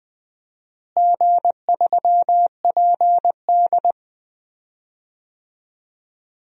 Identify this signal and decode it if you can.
Morse code, heard as a keyed tone, about 20 wpm, 700 Hz. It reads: G3PD
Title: DXCC-100 Call Signs - 4 Characters - Repeated 2x - 20wpm